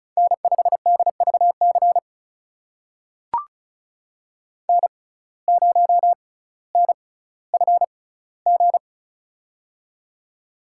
Morse code, heard as a keyed tone, about 34 wpm, spaced as for 18 wpm, 700 Hz. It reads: N5DVC N0NFG
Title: USA Call Signs - 5 Characters - 6x Character Spacing - 35wpm